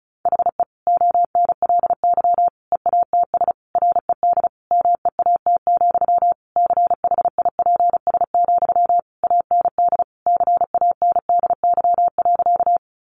Morse code, heard as a keyed tone, about 34 wpm, 700 Hz. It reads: HE ONLY EATS RED MEAT, CHIPS, AND CANDY.
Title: Easy Phrases 35wpm